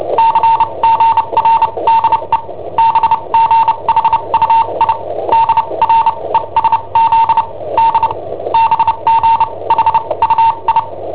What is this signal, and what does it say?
Morse code, heard as a keyed tone, about 29 wpm, spaced as for 23 wpm, 910 Hz, with noise. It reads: CGRDE BGHUI DRESZD BGHUI
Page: Cassetta esercizi preparata da I7OHP per IW5ELP